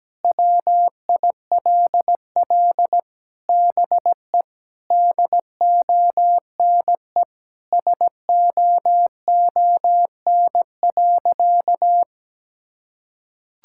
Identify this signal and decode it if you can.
Morse code, heard as a keyed tone, about 17 wpm, 700 Hz. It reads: WILL BE DONE SOON.
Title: Sentences from Top 1600 Words 17wpm